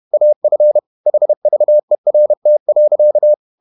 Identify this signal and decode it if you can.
Morse code, heard as a keyed tone, about 31 wpm, 600 Hz. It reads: AF HVERT.
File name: Ferie QRQ 31wpm